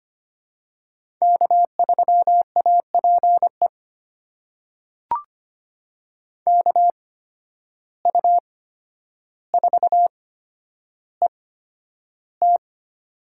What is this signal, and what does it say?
Morse code, heard as a keyed tone, about 25 wpm, spaced as for 11 wpm, 700 Hz. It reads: K3APE KU4ET
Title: USA Call Signs - 5 Characters - 8x Character Spacing - 25wpm